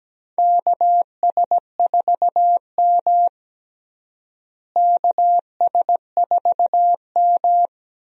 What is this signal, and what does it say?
Morse code, heard as a keyed tone, about 17 wpm, 700 Hz. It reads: KS4M KS4M
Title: USA Call Signs - 4 Characters - Repeated 3x - 17wpm